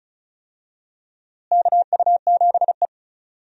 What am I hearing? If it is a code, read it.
Morse code, heard as a keyed tone, about 34 wpm, 700 Hz. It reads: KU7E
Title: USA Call Signs - 4 Characters - Repeated 2x - 35wpm